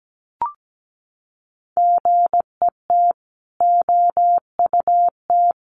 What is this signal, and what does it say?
Morse code, heard as a keyed tone, about 17 wpm, 700 Hz. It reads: GET OUT
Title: Sets of 5 Words Encore2 17wpm